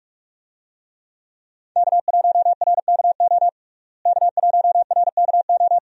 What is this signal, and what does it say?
Morse code, heard as a keyed tone, about 46 wpm, 700 Hz. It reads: K1RKO K1RKO
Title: USA Call Signs - 5 Characters - Repeated 2x - 45wpm